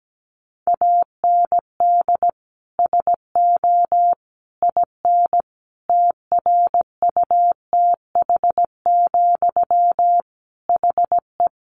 Morse code, heard as a keyed tone, about 17 wpm, 700 Hz. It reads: AND SO IN TRUTH, HE
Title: Sentences from Top 800 Words 17wpm